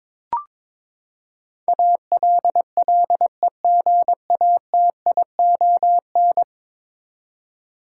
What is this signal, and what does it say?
Morse code, heard as a keyed tone, about 22 wpm, 700 Hz. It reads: ALLEGATION
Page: morse-code-ninja-practice-22wpm